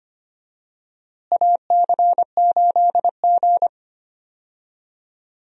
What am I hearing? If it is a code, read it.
Morse code, heard as a keyed tone, about 25 wpm, 700 Hz. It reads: AC8G
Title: USA Call Signs - 4 Characters - Repeated 2x - 25wpm